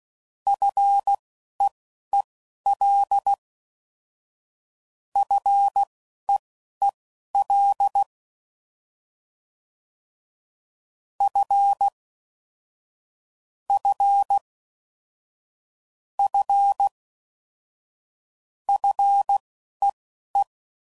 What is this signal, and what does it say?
Morse code, heard as a keyed tone, about 16 wpm, spaced as for 11 wpm, 800 Hz. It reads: FEEL FEEL F F F FEE
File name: Prefixes / FLs & THs / F Prefixes